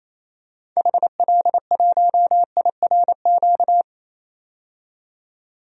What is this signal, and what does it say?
Morse code, heard as a keyed tone, about 28 wpm, 700 Hz. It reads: HL1IRQ
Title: DXCC-100 Call Signs - 6 Characters - 28wpm